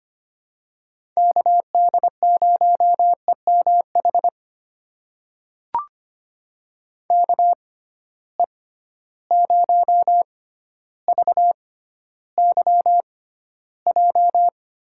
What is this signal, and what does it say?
Morse code, heard as a keyed tone, about 25 wpm, spaced as for 13 wpm, 700 Hz. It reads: KD0EMH KE0VYJ
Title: USA Call Signs - 6 Characters - 6x Character Spacing - 25wpm